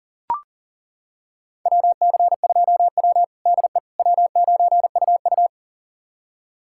Morse code, heard as a keyed tone, about 37 wpm, 700 Hz. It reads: WC2W DE W9UU
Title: USA Callsign DE Callsign - 4 Characters - 40wpm